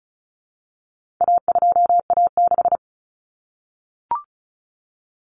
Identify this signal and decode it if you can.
Morse code, heard as a keyed tone, about 34 wpm, 700 Hz. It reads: A2A6
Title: Common Cut Numbers - 4-Digit 35wpm